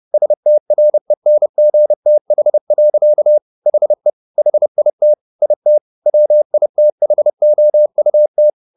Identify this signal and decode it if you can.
Morse code, heard as a keyed tone, about 30 wpm, 600 Hz. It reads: STRENGTH. HE HIT IT WITHOUT